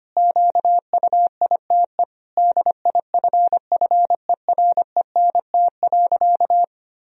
Morse code, heard as a keyed tone, about 25 wpm, 700 Hz. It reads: QUITE DIFFERENT.
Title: Sentences from Top 1000 Words - Review Part 3 - 25wpm